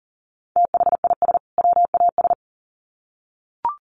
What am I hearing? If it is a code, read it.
Morse code, heard as a keyed tone, about 40 wpm, 700 Hz. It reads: THIS WAS
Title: Sets of 2 Words 40wpm